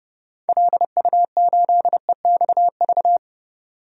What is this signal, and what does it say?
Morse code, heard as a keyed tone, about 30 wpm, 700 Hz. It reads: LU8EXV
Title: DXCC-100 Call Signs - 6 Characters - 30wpm